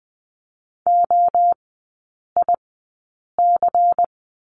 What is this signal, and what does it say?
Morse code, heard as a keyed tone, about 20 wpm, spaced as for 8 wpm, 700 Hz. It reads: OIC
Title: 184 Lesson 30e - Letter b - Three Characters 20wpm